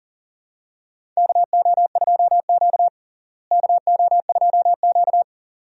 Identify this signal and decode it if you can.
Morse code, heard as a keyed tone, about 40 wpm, 700 Hz. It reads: KO2Q KO2Q
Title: USA Call Signs - 4 Characters - Repeated 2x - 40wpm